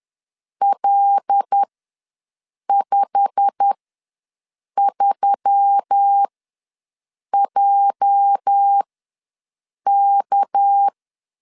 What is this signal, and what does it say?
Morse code, heard as a keyed tone, about 10 wpm, spaced as for 6 wpm, 800 Hz. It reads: L53JK
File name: LEZIONI / 10 SIGNS / 10A SLASH ws 10wpm 4121